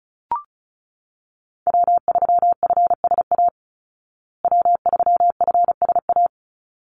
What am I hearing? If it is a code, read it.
Morse code, heard as a keyed tone, about 34 wpm, 700 Hz. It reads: W3FSA W3FSA
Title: Call Signs Repeated 3x - 35wpm